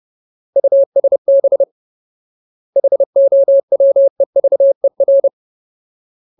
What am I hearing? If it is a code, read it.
Morse code, heard as a keyed tone, about 30 wpm, 550 Hz. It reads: USB HOWEVER
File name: Ultimate Word List / 30WPM Ultimate